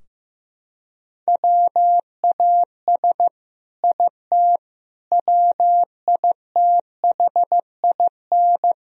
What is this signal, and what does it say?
Morse code, heard as a keyed tone, about 15 wpm, 700 Hz. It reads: WAS IT WITHIN
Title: Sentences from Top 1000 Words - Review Part 4 - 15wpm